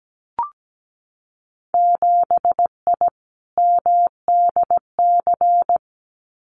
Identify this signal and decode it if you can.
Morse code, heard as a keyed tone, about 17 wpm, 700 Hz. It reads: 7I MDC
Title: Winter Field Day Exchanges Encore 17wpm